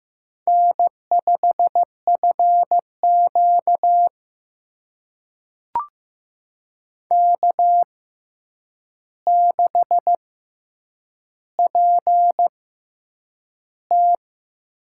Morse code, heard as a keyed tone, about 15 wpm, spaced as for 8 wpm, 700 Hz. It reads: N5FQ K6PT
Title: USA Call Signs - 4 Characters - 6x Character Spacing - 15wpm